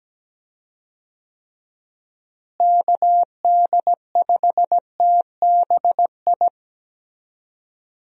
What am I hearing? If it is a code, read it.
Morse code, heard as a keyed tone, about 17 wpm, 700 Hz. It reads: KD5TBI
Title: USA Call Signs - 6 Characters - Repeated 2x - 17wpm